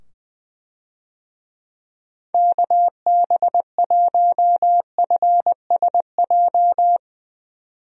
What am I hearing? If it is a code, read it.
Morse code, heard as a keyed tone, about 20 wpm, 700 Hz. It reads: KB1FSJ
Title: USA Call Signs - 6 Characters - 20wpm